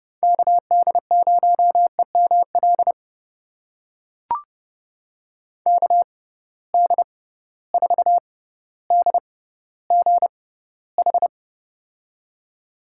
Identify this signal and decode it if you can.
Morse code, heard as a keyed tone, about 30 wpm, spaced as for 15 wpm, 700 Hz. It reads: KD0EML KD4DGH
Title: USA Call Signs - 6 Characters - 6x Character Spacing - 30wpm